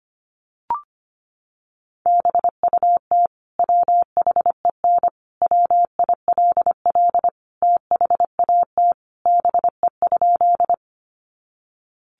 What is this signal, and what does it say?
Morse code, heard as a keyed tone, about 25 wpm, 700 Hz. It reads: BUT WHEN WILL THAT BE?
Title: Sentences from Top 100 Words 25wpm